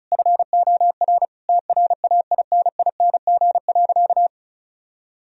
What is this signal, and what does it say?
Morse code, heard as a keyed tone, about 36 wpm, 700 Hz. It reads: FOR TRAINING.
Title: Sentences from Top 1000 Words - Review Part 4 - 35wpm